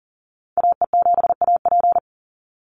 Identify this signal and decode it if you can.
Morse code, heard as a keyed tone, about 40 wpm, 700 Hz. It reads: AE7AP
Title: NA SOTA Chaser Callsigns 40wpm